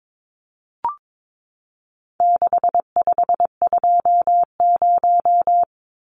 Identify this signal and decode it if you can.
Morse code, heard as a keyed tone, about 22 wpm, 700 Hz. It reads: 6520
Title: Numbers - 4-Digit 22wpm